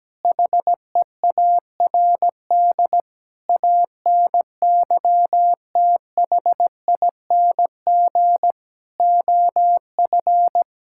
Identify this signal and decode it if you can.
Morse code, heard as a keyed tone, about 17 wpm, 700 Hz. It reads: HEARD ANYTHING OF
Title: Sentences from Top 1600 Words 17wpm